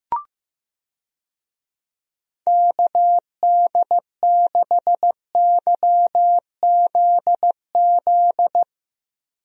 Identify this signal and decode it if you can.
Morse code, heard as a keyed tone, about 15 wpm, 700 Hz. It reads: KD6YZZ
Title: USA Call Signs - 6 Characters - Repeated 3x - 15wpm